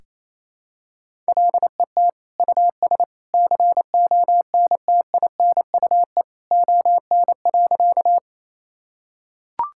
Morse code, heard as a keyed tone, about 27 wpm, 700 Hz. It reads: LET US CONTINUE ON.
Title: Sentences from Top 1000 Words - Review Part 6 - 28wpm